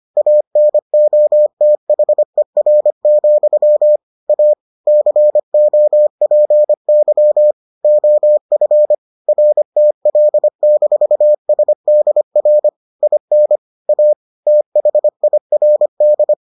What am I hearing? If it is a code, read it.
Morse code, heard as a keyed tone, about 25 wpm, 600 Hz. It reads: ANOTHER, A COPY OF RTL-SDR IN A THIRD